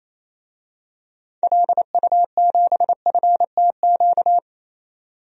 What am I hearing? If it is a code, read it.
Morse code, heard as a keyed tone, about 28 wpm, 700 Hz. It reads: LU7FTQ
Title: DXCC-100 Call Signs - 6 Characters - 28wpm